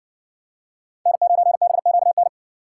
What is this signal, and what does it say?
Morse code, heard as a keyed tone, about 60 wpm, 700 Hz. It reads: N9BQN